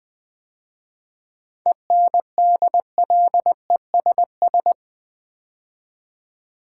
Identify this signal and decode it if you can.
Morse code, heard as a keyed tone, about 20 wpm, 700 Hz. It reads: ENDLESS
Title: 7 Letter Words - First 3 Letters Given - 20wpm